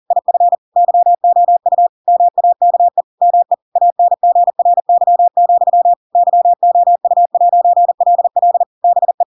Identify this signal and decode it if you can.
Morse code, heard as a keyed tone, about 40 wpm, 700 Hz. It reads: IF YOU MAKE ME ANGRY, YOU'LL BE
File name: Pinocchio / Pinocchio ch4 40 40 ik1hge